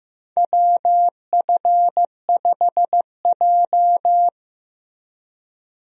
Morse code, heard as a keyed tone, about 15 wpm, 700 Hz. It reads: WF5J
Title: USA Call Signs - 4 Characters - Repeated 2x - 15wpm